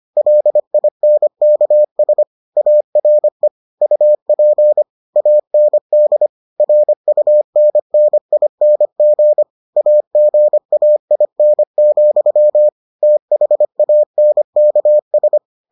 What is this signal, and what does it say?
Morse code, heard as a keyed tone, about 25 wpm, 600 Hz. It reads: LINKS ARE UP AND RUNNING AGAIN, THANKS